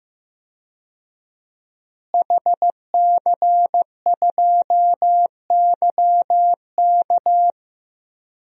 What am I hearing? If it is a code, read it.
Morse code, heard as a keyed tone, about 15 wpm, 700 Hz. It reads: HC2YK
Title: DXCC-100 Call Signs - 5 Characters - Repeated 2x - 15wpm